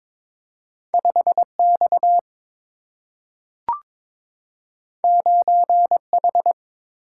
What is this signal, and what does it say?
Morse code, heard as a keyed tone, about 22 wpm, 700 Hz. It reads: 5 X 9 H
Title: Single-Letter Single-Number 22wpm